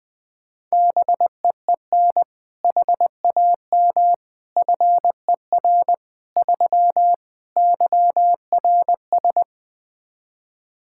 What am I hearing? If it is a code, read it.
Morse code, heard as a keyed tone, about 20 wpm, 700 Hz. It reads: BEEN HAM FER 3 YRS
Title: Practice QSO - Exchange 3 - More Info - 20wpm